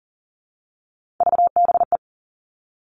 Morse code, heard as a keyed tone, about 40 wpm, 700 Hz. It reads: VBE
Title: All Cut Numbers - 3-Digit 40wpm